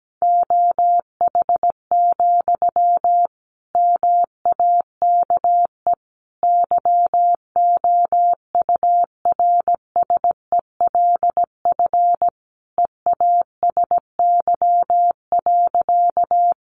Morse code, heard as a keyed tone, about 17 wpm, 700 Hz. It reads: OH, MAKE YOURSELF EASY.
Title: Sentences from Top 700 Words 17wpm